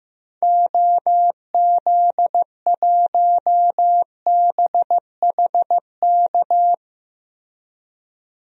Morse code, heard as a keyed tone, about 15 wpm, 700 Hz. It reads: OZ1BHK
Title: DXCC-100 Call Signs - 6 Characters - Repeated 3x - 15wpm